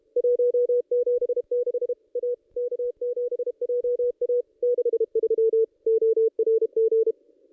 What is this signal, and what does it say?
Morse code, heard as a keyed tone, about 28 wpm, 440 Hz. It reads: 176 A K7JA 63 ORG
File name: contests / ss-cw-k5tr-op-n5rz -1800